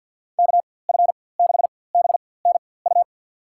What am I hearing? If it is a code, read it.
Morse code, heard as a keyed tone, about 50 wpm, spaced as for 23 wpm, 700 Hz. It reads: KF6BNU
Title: USA Call Signs - 6 Characters - 4x Character Spacing - 50wpm